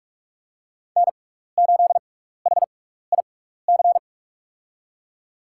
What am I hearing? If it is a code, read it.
Morse code, heard as a keyed tone, about 47 wpm, spaced as for 16 wpm, 700 Hz. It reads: N8HIC
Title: USA Call Signs - 5 Characters - 6x Character Spacing - 45wpm